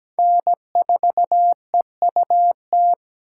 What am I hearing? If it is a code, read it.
Morse code, heard as a keyed tone, about 17 wpm, 700 Hz. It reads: N4EUT
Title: USA Call Signs - 5 Characters - 2x Character Spacing - 17wpm